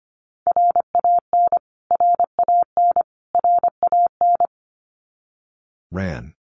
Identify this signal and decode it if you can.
Morse code, heard as a keyed tone, about 25 wpm, 700 Hz. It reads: RAN RAN RAN
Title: Top 500 Words Repeated 25wpm